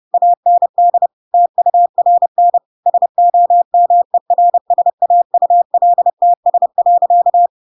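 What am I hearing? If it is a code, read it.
Morse code, heard as a keyed tone, about 30 wpm, 700 Hz. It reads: AND TURN SOMERSAULTS.
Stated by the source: Chapter 2 of The Adventures of Pinocchio read as a book in Morse code (CW) at 30 wpm, by ik1hge.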